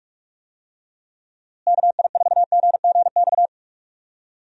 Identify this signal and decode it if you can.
Morse code, heard as a keyed tone, about 46 wpm, 700 Hz. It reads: KI4GGX
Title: USA Call Signs - 6 Characters - Repeated 3x - 45wpm